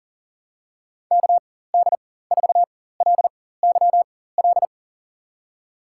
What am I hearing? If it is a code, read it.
Morse code, heard as a keyed tone, about 40 wpm, spaced as for 19 wpm, 700 Hz. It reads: KD4LYL
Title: USA Call Signs - 6 Characters - 4x Character Spacing - 40wpm